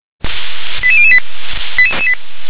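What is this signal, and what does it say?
Five-tone selective call (ZVEI), heard as quick runs of five tones, tones of about 70 ms. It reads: 88008
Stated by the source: ZVEI five-tone selective call received by SDR.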